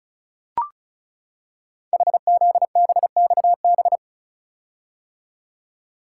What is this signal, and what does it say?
Morse code, heard as a keyed tone, about 36 wpm, 700 Hz. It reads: HZBXB
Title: Code Groups - 5 Letters - 35wpm